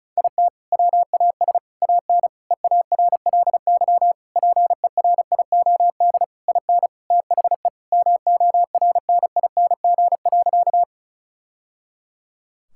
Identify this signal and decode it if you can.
Morse code, heard as a keyed tone, about 36 wpm, 700 Hz. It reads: IT WAS AN EARLY PERIOD IN THE MORNING.
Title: Sentences from Top 1000 Words - Review Part 5 - 35wpm